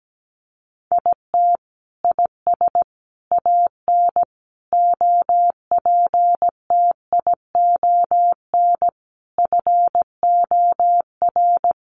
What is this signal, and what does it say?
Morse code, heard as a keyed tone, about 17 wpm, 700 Hz. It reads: IT IS AN OPTION FOR
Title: Sentences from Top 1300 Words 17wpm